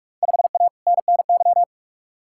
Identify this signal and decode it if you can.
Morse code, heard as a keyed tone, about 47 wpm, 700 Hz. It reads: 5A NNY
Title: Field Day Exchanges - 45wpm